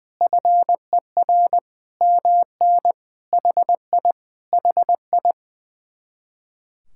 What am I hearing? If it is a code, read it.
Morse code, heard as a keyed tone, about 21 wpm, 700 Hz. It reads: FER MN HI HI
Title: Speed-Racing Common QSO Phrases - 20wpm